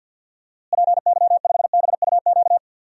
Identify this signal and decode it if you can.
Morse code, heard as a keyed tone, about 51 wpm, 700 Hz. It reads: PY5BUQ
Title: DXCC-100 Call Signs - 6 Characters - Repeated 3x - 50wpm